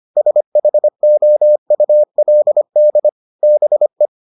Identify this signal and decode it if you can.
Morse code, heard as a keyed tone, about 25 wpm, 600 Hz. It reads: SHOULD BE